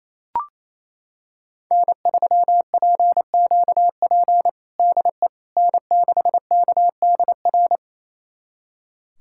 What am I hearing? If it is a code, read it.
Morse code, heard as a keyed tone, about 28 wpm, 700 Hz. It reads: N3PQP DE N6KDR
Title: USA Callsign DE Callsign - 5 Characters - 28wpm